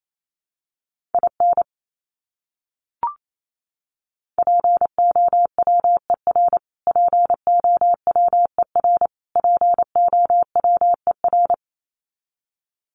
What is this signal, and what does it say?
Morse code, heard as a keyed tone, about 28 wpm, 700 Hz. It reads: IN POWER POWER POWER
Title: Top 500 Words Repeated 28wpm